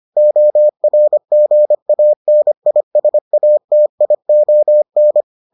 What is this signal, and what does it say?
Morse code, heard as a keyed tone, about 25 wpm, 600 Hz. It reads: ORGANISATION